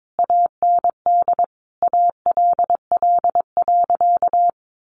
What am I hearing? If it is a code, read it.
Morse code, heard as a keyed tone, about 22 wpm, 700 Hz. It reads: AND ALL.
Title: Sentences from Top 200 Words 22wpm